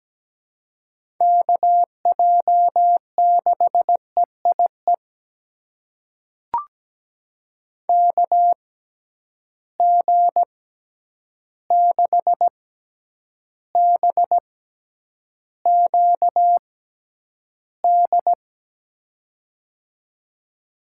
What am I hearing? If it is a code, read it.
Morse code, heard as a keyed tone, about 17 wpm, spaced as for 9 wpm, 700 Hz. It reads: KJ6EIE KG6BQD
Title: USA Call Signs - 6x Character Spacing - 17wpm